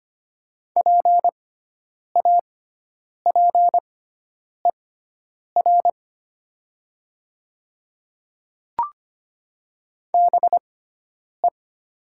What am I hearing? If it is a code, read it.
Morse code, heard as a keyed tone, about 25 wpm, spaced as for 9 wpm, 700 Hz. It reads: PAPER BE
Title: 6x Character Spacing - Top 300 Words - 25wpm